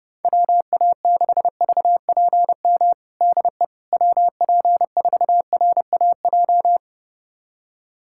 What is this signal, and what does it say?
Morse code, heard as a keyed tone, about 30 wpm, 700 Hz. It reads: WA6VPM DE WP4RAJ
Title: USA Callsign DE Callsign - 6 Characters - 30wpm